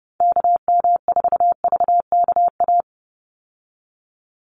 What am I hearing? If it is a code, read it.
Morse code, heard as a keyed tone, about 30 wpm, 700 Hz. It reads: KM4VKA
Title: USA Call Signs 30wpm